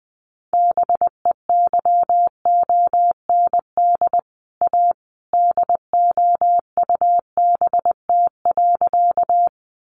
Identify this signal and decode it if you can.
Morse code, heard as a keyed tone, about 19 wpm, 700 Hz. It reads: BEYOND A DOUBT.
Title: Sentences from Top 1500 Words 20wpm